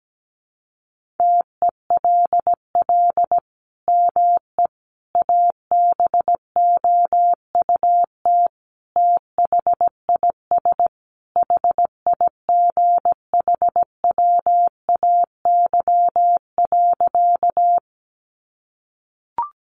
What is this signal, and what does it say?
Morse code, heard as a keyed tone, about 17 wpm, 700 Hz. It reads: TELL ME ABOUT THIS HIGHWAY.
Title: Sentences from Top 1400 Words 17wpm